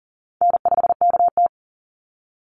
Morse code, heard as a keyed tone, about 40 wpm, 700 Hz. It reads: N5KT
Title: Call Signs Repeated 3x - 40wpm